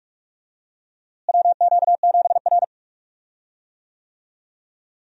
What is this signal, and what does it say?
Morse code, heard as a keyed tone, about 41 wpm, 700 Hz. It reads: WQ7R
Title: USA Call Signs - 4 Characters - Repeated 3x - 45wpm